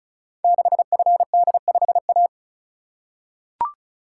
Morse code, heard as a keyed tone, about 36 wpm, 700 Hz. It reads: 6FD5A
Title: Code Groups - 5 Letter:Numbers - 35wpm